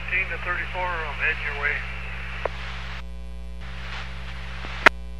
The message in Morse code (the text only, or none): none